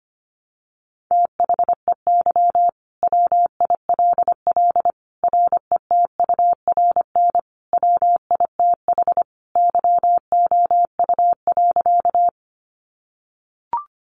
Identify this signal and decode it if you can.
Morse code, heard as a keyed tone, about 25 wpm, 700 Hz. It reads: THEY WILL RETURN WITH YOU.
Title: Sentences from Top 800 Words 25wpm